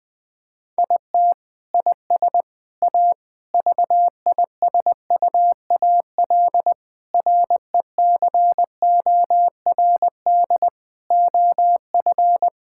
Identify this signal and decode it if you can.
Morse code, heard as a keyed tone, about 20 wpm, 700 Hz. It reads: IT IS A VISUAL RECORD OF
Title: Sentences from Top 1000 Words - Review Part 6 - 20wpm